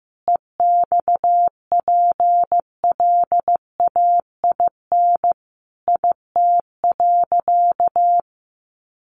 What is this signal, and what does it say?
Morse code, heard as a keyed tone, about 15 wpm, 700 Hz. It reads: EXPLAIN IT.
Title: Sentences from Top 1100 Words 15wpm